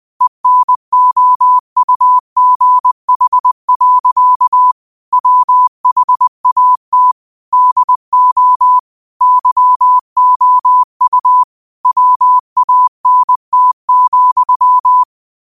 Morse code, heard as a keyed tone, about 20 wpm, 1000 Hz. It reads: ENOUGH. WHAT DO YOU WANT,
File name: qotdfiles / 20wpm